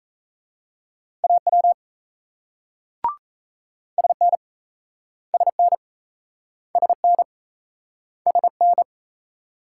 Morse code, heard as a keyed tone, about 37 wpm, 700 Hz. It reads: AW SN SN SN SN